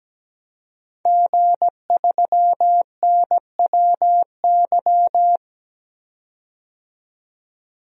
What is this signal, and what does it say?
Morse code, heard as a keyed tone, about 17 wpm, 700 Hz. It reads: G3NWY
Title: DXCC-100 Call Signs - 5 Characters - 17wpm